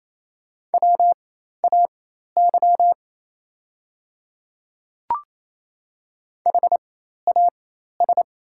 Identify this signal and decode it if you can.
Morse code, heard as a keyed tone, about 28 wpm, spaced as for 13 wpm, 700 Hz. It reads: WAY HAS
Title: 4x Character Spacing - Top 100 Words - 28wpm